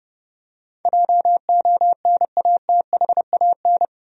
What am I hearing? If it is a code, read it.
Morse code, heard as a keyed tone, about 32 wpm, 700 Hz. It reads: JONATHAN
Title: USA Common Names - 30wpm